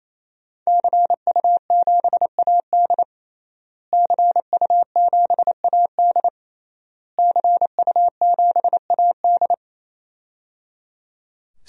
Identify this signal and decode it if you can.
Morse code, heard as a keyed tone, about 28 wpm, 700 Hz. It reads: CU7AD CU7AD CU7AD
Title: DXCC-100 Call Signs - 5 Characters - Repeated 3x - 28wpm